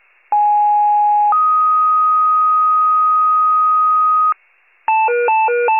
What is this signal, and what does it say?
Two-tone page, heard as a one-second tone, then a three-second tone, about 830 then 1250 Hz. It is heard from 0.3 to 4.3 s.